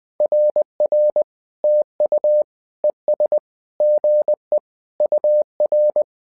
Morse code, heard as a keyed tone, about 20 wpm, 600 Hz. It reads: RR TU ES GE UR
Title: SOTA QSO Practice 20wpm